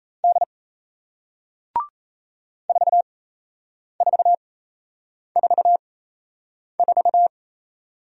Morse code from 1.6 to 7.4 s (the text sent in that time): 4444